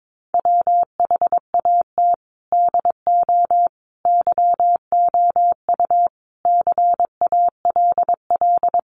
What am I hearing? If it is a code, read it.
Morse code, heard as a keyed tone, about 22 wpm, 700 Hz. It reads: WHAT DO YOU CALL
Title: Sentences from Top 1400 Words 22wpm